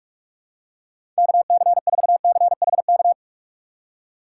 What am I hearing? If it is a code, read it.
Morse code, heard as a keyed tone, about 45 wpm, 700 Hz. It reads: KC4CHK
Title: USA Call Signs - 2x Character Spacing - 45wpm